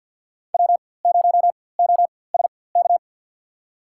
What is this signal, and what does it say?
Morse code, heard as a keyed tone, about 53 wpm, spaced as for 24 wpm, 700 Hz. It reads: W0OSK
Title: USA Call Signs - 4x Character Spacing - 50wpm